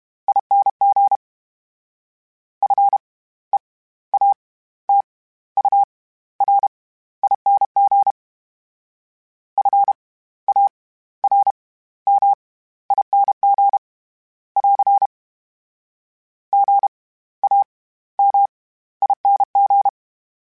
Morse code, heard as a keyed tone, about 32 wpm, spaced as for 16 wpm, 800 Hz. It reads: ING FEATURING FARMING <AR> GAMING
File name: Special INGs / Special INGs-mp3s / SlowToFaster ing @30